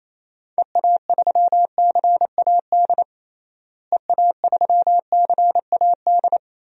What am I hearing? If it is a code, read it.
Morse code, heard as a keyed tone, about 28 wpm, 700 Hz. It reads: EA3CAD EA3CAD
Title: DXCC-100 Call Signs - 6 Characters - Repeated 3x - 28wpm